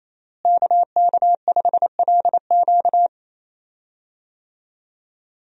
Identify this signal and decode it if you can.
Morse code, heard as a keyed tone, about 27 wpm, 700 Hz. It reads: KK5LQ
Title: USA Call Signs - 5 Characters - Repeated 2x - 28wpm